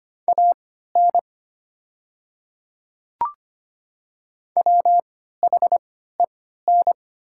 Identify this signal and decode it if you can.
Morse code, heard as a keyed tone, about 25 wpm, spaced as for 14 wpm, 700 Hz. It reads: AN WHEN
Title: 3x Character Spacing - Top 300 Words - 25wpm